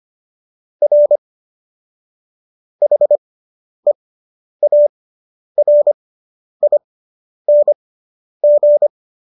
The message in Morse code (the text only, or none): R HEARING